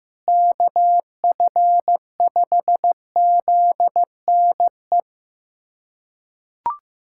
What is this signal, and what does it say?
Morse code, heard as a keyed tone, about 15 wpm, 700 Hz. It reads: KF5ZNE
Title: USA Callsign DE Callsign 15wpm